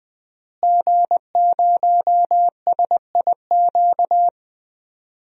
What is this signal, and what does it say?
Morse code, heard as a keyed tone, about 20 wpm, 700 Hz. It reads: G0SIQ
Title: DXCC-100 Call Signs - 5 Characters - Repeated 2x - 20wpm